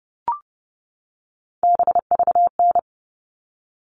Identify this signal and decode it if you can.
Morse code, heard as a keyed tone, about 30 wpm, 700 Hz. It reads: BVN
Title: All Cut Numbers - 3-Digit 30wpm